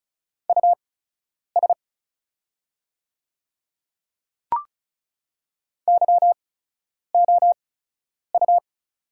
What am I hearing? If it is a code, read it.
Morse code, heard as a keyed tone, about 34 wpm, spaced as for 10 wpm, 700 Hz. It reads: US YOU
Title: 8x Character Spacing - Top 200 Words - 35wpm